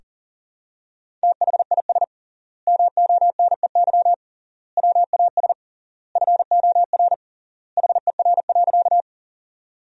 Morse code, heard as a keyed tone, about 40 wpm, 700 Hz. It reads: THIS MONEY WAS FOR HER.
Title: 3x Word-Spacing - Sentences from Top 400 Words - 40wpm